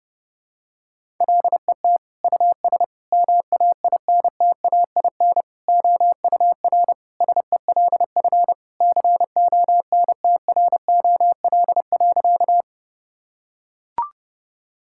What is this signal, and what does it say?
Morse code, heard as a keyed tone, about 30 wpm, 700 Hz. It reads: LET US MAINTAIN OUR SELF CONTROL.
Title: Sentences from Top 1000 Words - Review Part 5 - 30wpm